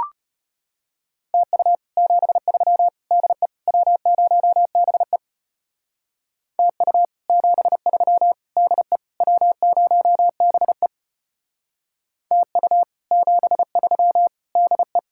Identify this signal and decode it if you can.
Morse code, heard as a keyed tone, about 32 wpm, 700 Hz. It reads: TU 73 DE W0BE TU 73 DE W0BE TU 73 DE